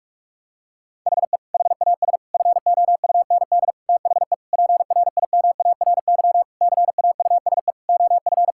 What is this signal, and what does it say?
Morse code, heard as a keyed tone, about 46 wpm, 700 Hz. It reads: HE HAS FOUND THE PRIMARY CAUSE OF
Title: Sentences from Top 1000 Words - Review Part 5 - 45wpm